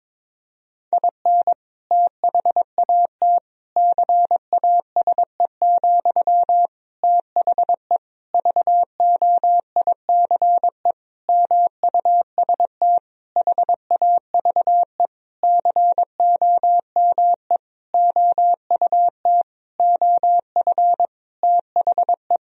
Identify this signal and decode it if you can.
Morse code, heard as a keyed tone, about 21 wpm, 700 Hz. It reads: IN THAT CASE, THE VOICE MUST HAVE COME OUT OF THE
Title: Sentences from Top 1000 Words - Review Part 2 - 22wpm